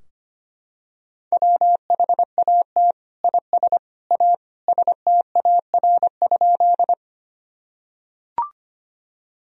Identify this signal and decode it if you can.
Morse code, heard as a keyed tone, about 25 wpm, 700 Hz. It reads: WHAT IS A STAR?
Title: Sentences from Top 1000 Words - Review Part 2 - 25wpm